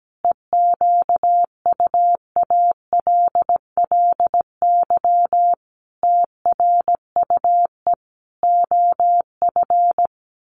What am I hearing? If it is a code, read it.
Morse code, heard as a keyed tone, about 17 wpm, 700 Hz. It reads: EQUALLY TRUE OF
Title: Sentences from Top 1400 Words 17wpm